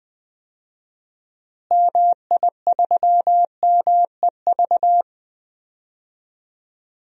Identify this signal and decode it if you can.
Morse code, heard as a keyed tone, about 20 wpm, 700 Hz. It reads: MI3MEV
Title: DXCC-100 Call Signs - 6 Characters - Repeated 2x - 20wpm